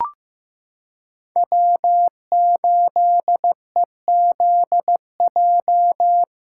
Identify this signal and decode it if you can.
Morse code, heard as a keyed tone, about 15 wpm, 700 Hz. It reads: W8EZJ
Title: USA Callsign DE Callsign 15wpm